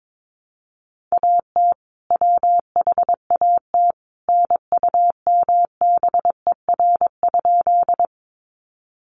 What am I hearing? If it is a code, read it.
Morse code, heard as a keyed tone, about 22 wpm, 700 Hz. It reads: AT WHAT NUMBER?
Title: Sentences from Top 100 Words 22wpm